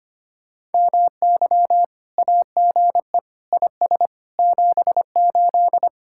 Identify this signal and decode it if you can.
Morse code, heard as a keyed tone, about 31 wpm, 700 Hz. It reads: MY AGE IS 78
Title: Speed-Racing Common QSO Phrases - 25wpm